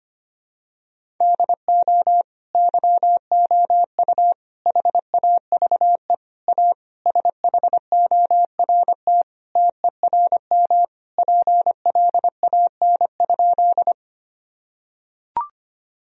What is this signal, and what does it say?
Morse code, heard as a keyed tone, about 25 wpm, 700 Hz. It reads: DO YOU HAVE A SHORT TERM PLAN?
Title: Sentences from Top 1000 Words - Review Part 4 - 25wpm